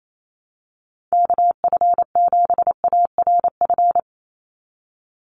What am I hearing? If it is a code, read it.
Morse code, heard as a keyed tone, about 28 wpm, 700 Hz. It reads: KF7ARF
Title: USA Call Signs 28wpm